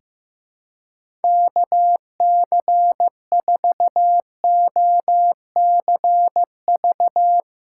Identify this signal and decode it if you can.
Morse code, heard as a keyed tone, about 15 wpm, 700 Hz. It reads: KC4OCV
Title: USA Call Signs - 6 Characters - 6x Character Spacing - 15wpm